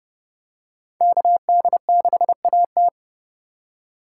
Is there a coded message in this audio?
Yes: Morse code from 1.0 to 2.9 s, heard as a keyed tone, about 29 wpm, 700 Hz. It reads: KD6AT